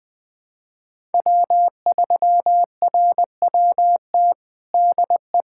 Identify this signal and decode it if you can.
Morse code, heard as a keyed tone, about 20 wpm, 700 Hz. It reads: W3RWT DE
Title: USA Callsign DE Callsign - 5 Characters - 20wpm